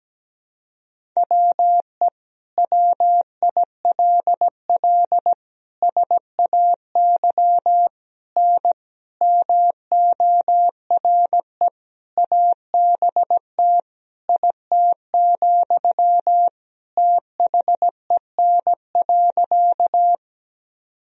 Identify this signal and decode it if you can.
Morse code, heard as a keyed tone, about 17 wpm, 700 Hz. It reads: WE WILL SAY N MORE ABT IT, THEN.
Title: Sentences from Top 200 Words and Abbreviations 17wpm